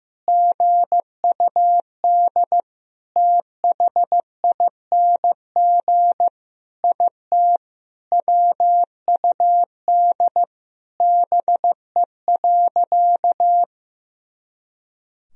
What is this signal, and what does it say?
Morse code, heard as a keyed tone, about 15 wpm, 700 Hz. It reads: GUD THING IT WUD BE.
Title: Sentences from Top 200 Words and Abbreviations 15wpm